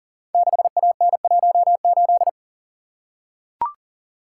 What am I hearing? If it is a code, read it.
Morse code, heard as a keyed tone, about 40 wpm, 700 Hz. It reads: 6AN18
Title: Code Groups - 5 Letter:Numbers - 40wpm